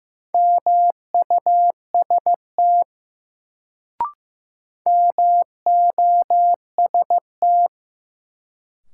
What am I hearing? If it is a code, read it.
Morse code, heard as a keyed tone, about 15 wpm, 700 Hz. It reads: MUST MOST
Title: Sets of 2 Words - New Word Drill - 15wpm